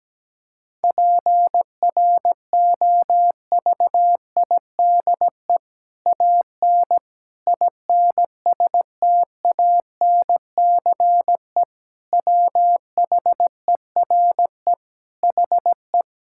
Morse code, heard as a keyed tone, about 17 wpm, 700 Hz. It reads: PROVIDE AN INSTANCE WHERE HE
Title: Sentences from Top 1000 Words - Review Part 5 - 17wpm